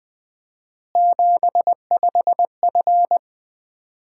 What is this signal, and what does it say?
Morse code, heard as a keyed tone, about 20 wpm, 700 Hz. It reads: 75F
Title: Common QSO Phrases - 4x Word-Spacing - 20wpm